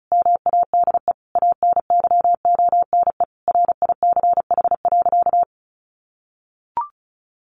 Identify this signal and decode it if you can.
Morse code, heard as a keyed tone, about 34 wpm, 700 Hz. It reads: MADE ANYONE RICH.
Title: Sentences from Top 1100 Words 35wpm